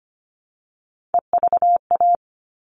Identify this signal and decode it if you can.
Morse code, heard as a keyed tone, about 25 wpm, 700 Hz. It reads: EVA
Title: All Cut Numbers - 3-Digit 25wpm